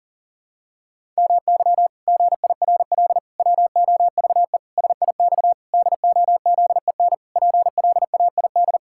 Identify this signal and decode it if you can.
Morse code, heard as a keyed tone, about 37 wpm, 700 Hz. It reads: MY GIRL WOVE SIX DOZEN PLAID
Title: Pangram Challenge v2 - 40wpm